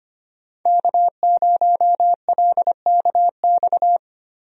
Morse code, heard as a keyed tone, about 25 wpm, 700 Hz. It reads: K0LKX
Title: USA Call Signs - 5 Characters - 25wpm